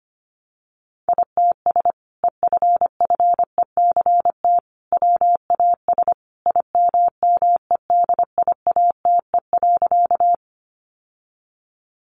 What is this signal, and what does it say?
Morse code, heard as a keyed tone, about 25 wpm, 700 Hz. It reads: ITS EFFECT WAS IMMEDIATE.
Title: Sentences from Top 1200 Words 25wpm